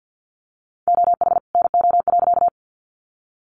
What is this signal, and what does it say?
Morse code, heard as a keyed tone, about 50 wpm, 700 Hz. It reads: OH NO.
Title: Additional Easy Phrases 50wpm